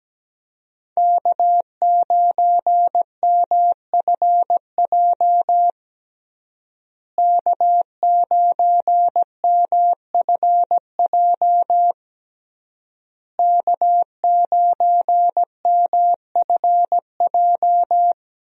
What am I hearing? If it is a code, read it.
Morse code, heard as a keyed tone, about 17 wpm, 700 Hz. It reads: K9MFJ K9MFJ K9MFJ
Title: USA Call Signs - 5 Characters - Repeated 3x - 17wpm